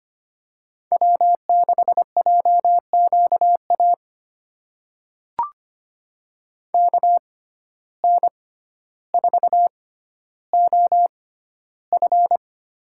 Morse code, heard as a keyed tone, about 25 wpm, spaced as for 13 wpm, 700 Hz. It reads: W6JQA KN4OF
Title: USA Call Signs - 6x Character Spacing - 25wpm